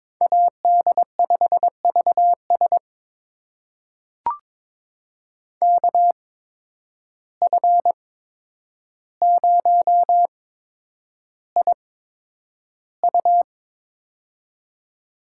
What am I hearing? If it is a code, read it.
Morse code, heard as a keyed tone, about 22 wpm, spaced as for 9 wpm, 700 Hz. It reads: AD5VS KF0IU
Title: USA Call Signs - 5 Characters - 8x Character Spacing - 22wpm